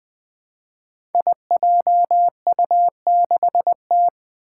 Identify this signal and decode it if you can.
Morse code, heard as a keyed tone, about 20 wpm, 700 Hz. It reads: IJU6T
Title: Five Challenging Letter-Number Combinations 20wpm